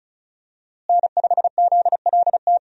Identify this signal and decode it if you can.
Morse code, heard as a keyed tone, about 34 wpm, 700 Hz. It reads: N5ZLT
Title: USA Call Signs - 5 Characters - Repeated 2x - 35wpm